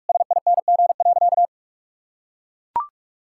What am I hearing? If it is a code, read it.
Morse code, heard as a keyed tone, about 41 wpm, 700 Hz. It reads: SING.
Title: Sentences from Top 1600 Words 45wpm